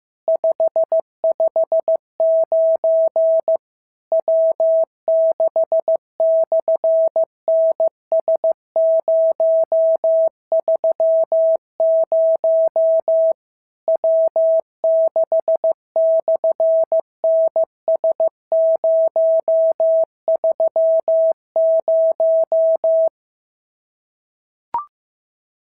Morse code, heard as a keyed tone, about 14 wpm, 650 Hz. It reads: 559 W6/NS030 W6/NS030
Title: SOTA QSO Practice 15wpm